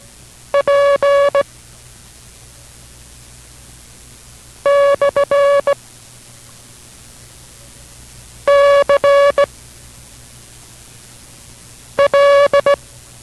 Morse code, heard as a keyed tone, about 18 wpm, 560 Hz. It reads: P / C L